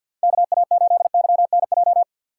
Morse code, heard as a keyed tone, about 53 wpm, 700 Hz. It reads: KA8YNJ
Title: USA Call Signs - 6 Characters - 4x Character Spacing - 50wpm